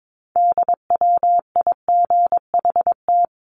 Morse code, heard as a keyed tone, about 22 wpm, 700 Hz. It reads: DWIGHT
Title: CWT Exchange 22wpm